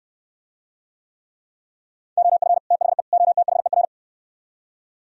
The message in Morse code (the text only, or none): OF THE OTHER